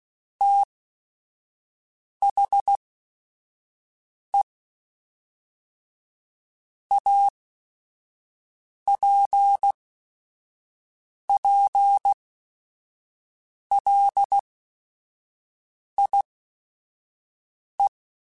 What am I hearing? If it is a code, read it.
Morse code, heard as a keyed tone, about 16 wpm, spaced as for 5 wpm, 800 Hz. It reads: THE APPLIE
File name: Technical Subjects / Technical Subjects-mp3s / XL @5